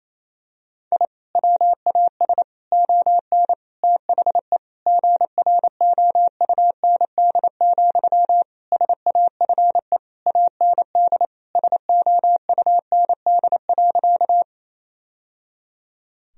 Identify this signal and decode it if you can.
Morse code, heard as a keyed tone, about 28 wpm, 700 Hz. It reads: I WAS ON THE GROUND, SAFE AND SOUND.
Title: Sentences from Top 1000 Words - Review Part 2 - 28wpm